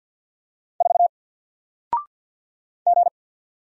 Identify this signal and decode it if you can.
Morse code, heard as a keyed tone, about 50 wpm, 700 Hz. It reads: 4 G
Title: Single Letter-Number 50wpm